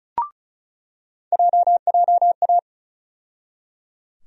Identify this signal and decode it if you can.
Morse code, heard as a keyed tone, about 36 wpm, 700 Hz. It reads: JJA
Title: USA Call Sign Postfixes - 3 Letters 35wpm